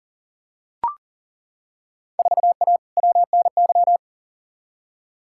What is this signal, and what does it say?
Morse code, heard as a keyed tone, about 39 wpm, 700 Hz. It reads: 4A WNY
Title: Field Day Exchanges Encore - 40wpm